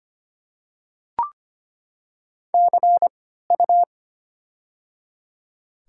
Morse code, heard as a keyed tone, about 25 wpm, spaced as for 14 wpm, 700 Hz. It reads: CU
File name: Top 90 QSO Elements - 3x Character Spacing - 25wpm